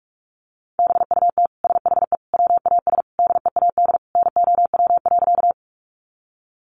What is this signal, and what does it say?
Morse code, heard as a keyed tone, about 44 wpm, 700 Hz. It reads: BUT SHE WAS DEAD NOW.
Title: Sentences from Top 1500 Words 45wpm